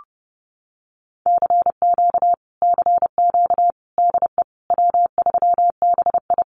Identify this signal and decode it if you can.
Morse code, heard as a keyed tone, about 30 wpm, 700 Hz. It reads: CQ CQ DE W3BI
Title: Common QSO Phrases Encore 30wpm